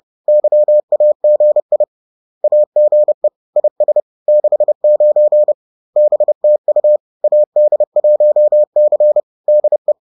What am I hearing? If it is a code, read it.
Morse code, heard as a keyed tone, about 30 wpm, 600 Hz. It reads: YAGI AGE IS 69 BTU AD1C DE